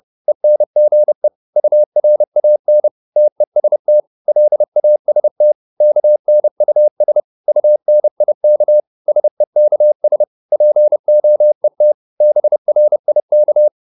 Morse code, heard as a keyed tone, about 30 wpm, 600 Hz. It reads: ENGE URAN TEST LAST KNUS UNIK SEKS POET BRIK
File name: Korte ord DK 30wpm